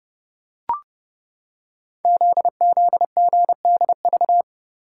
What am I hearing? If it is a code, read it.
Morse code, heard as a keyed tone, about 30 wpm, 700 Hz. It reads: ZZGDV
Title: Code Groups - 5 Letters - 30wpm